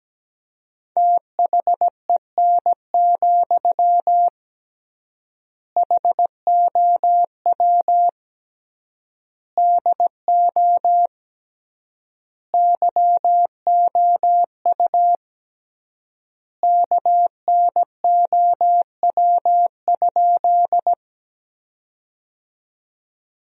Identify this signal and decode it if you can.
Morse code, heard as a keyed tone, about 17 wpm, 700 Hz. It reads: THEN, HOW DO YOU KNOW?
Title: 3x Word-Spacing - Sentences from Top 100 Words - 17wpm